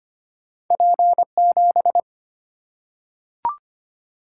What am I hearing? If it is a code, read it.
Morse code, heard as a keyed tone, about 25 wpm, 700 Hz. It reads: P 7
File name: Two Letter-Number Combinations - 25wpm